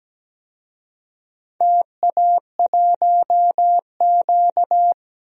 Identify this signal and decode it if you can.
Morse code, heard as a keyed tone, about 17 wpm, 700 Hz. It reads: TA1Q
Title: DXCC-100 Call Signs - 4 Characters - Repeated 2x - 17wpm